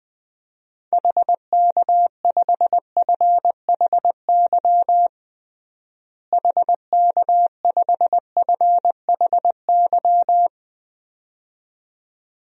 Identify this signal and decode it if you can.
Morse code, heard as a keyed tone, about 20 wpm, 700 Hz. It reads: HK5FHY HK5FHY
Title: DXCC-100 Call Signs - 6 Characters - Repeated 2x - 20wpm